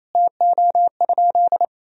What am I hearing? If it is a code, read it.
Morse code, heard as a keyed tone, about 28 wpm, 700 Hz. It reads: TO?
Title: Sentences from Top 200 Words and Abbreviations 28wpm